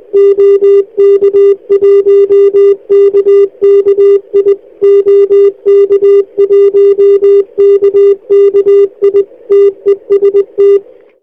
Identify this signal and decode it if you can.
Morse code, heard as a keyed tone, about 20 wpm, 400 Hz. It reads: OK1KKI OK1KKI TEST